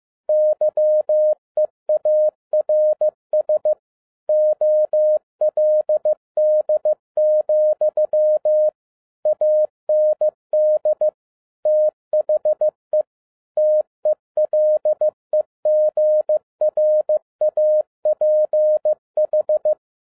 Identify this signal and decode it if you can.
Morse code, heard as a keyed tone, about 15 wpm, 610 Hz. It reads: YEARS OLD, AND THE TELEGRAPH